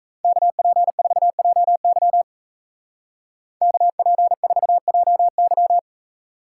Morse code, heard as a keyed tone, about 34 wpm, 700 Hz. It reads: KP4JY KP4JY
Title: Speed Racing - USA Call Signs 28wpm